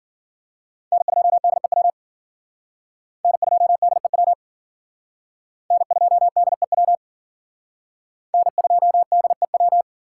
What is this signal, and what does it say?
Morse code, heard as a keyed tone, about 53 wpm, 700 Hz. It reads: N2DEW N2DEW N2DEW N2DEW